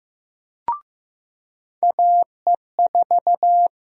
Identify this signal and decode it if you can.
Morse code, heard as a keyed tone, about 15 wpm, 700 Hz. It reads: AE4
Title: USA Call Sign Prefixes - 15wpm